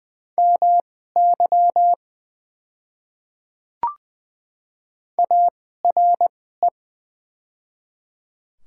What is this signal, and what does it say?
Morse code, heard as a keyed tone, about 20 wpm, spaced as for 14 wpm, 690 Hz. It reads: MY ARE
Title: 2x Character Spacing - Top 500 Words - 20wpm